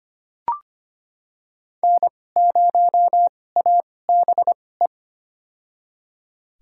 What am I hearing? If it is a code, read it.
Morse code, heard as a keyed tone, about 25 wpm, spaced as for 18 wpm, 700 Hz. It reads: N0ABE
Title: USA Call Signs - 5 Characters - 2x Character Spacing - 25wpm